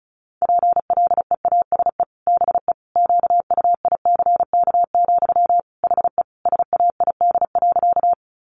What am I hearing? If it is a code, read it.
Morse code, heard as a keyed tone, about 34 wpm, 700 Hz. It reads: PLEASE BE QUICK, HE SAID.
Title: Sentences from Top 1500 Words 35wpm